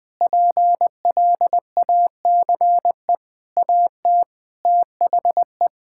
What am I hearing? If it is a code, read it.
Morse code, heard as a keyed tone, about 21 wpm, 700 Hz. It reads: PLACE AT THE
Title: Sentences with 9-Word Phrases - 20wpm